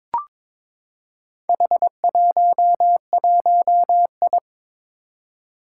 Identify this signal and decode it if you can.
Morse code, heard as a keyed tone, about 22 wpm, 700 Hz. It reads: H11I
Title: Four Challenging Letter-Number Combinations 22wpm